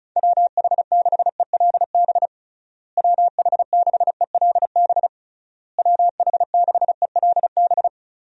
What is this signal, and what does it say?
Morse code, heard as a keyed tone, about 36 wpm, 700 Hz. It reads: WH6ELB WH6ELB WH6ELB
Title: USA Call Signs - 6 Characters - Repeated 3x - 35wpm